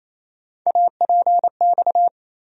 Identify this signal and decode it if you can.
Morse code, heard as a keyed tone, about 27 wpm, 700 Hz. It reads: APX
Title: USA Call Sign Postfixes - 3 Letters 28wpm